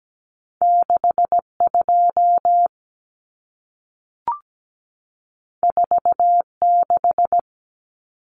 Morse code, heard as a keyed tone, about 17 wpm, 700 Hz. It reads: 62 46
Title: Numbers - 2-Digit 17wpm